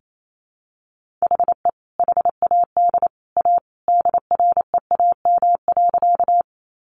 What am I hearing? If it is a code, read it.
Morse code, heard as a keyed tone, about 28 wpm, 700 Hz. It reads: HE HAD A DREAM.
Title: Sentences from Top 1500 Words 28wpm